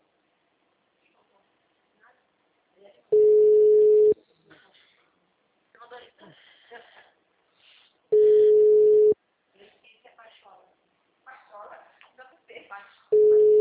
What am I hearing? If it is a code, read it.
Ringback tone (Europe, 425 Hz), with speech.
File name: out- -2033- - - .80146